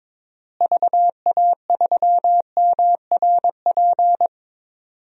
Morse code, heard as a keyed tone, about 21 wpm, 700 Hz. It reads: VA3MRP
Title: DXCC-100 Call Signs - 6 Characters - Repeated 2x - 22wpm